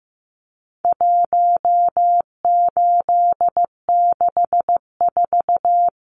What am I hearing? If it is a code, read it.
Morse code, heard as a keyed tone, about 15 wpm, 700 Hz. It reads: 1864
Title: Numbers - 4-Digit 15wpm